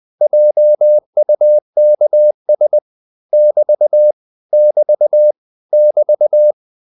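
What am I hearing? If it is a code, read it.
Morse code, heard as a keyed tone, about 20 wpm, 600 Hz. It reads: JUKS <BT> <BT> <BT>
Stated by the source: Morse code practice of short Danish words at 20 wpm, from CW med Gnister.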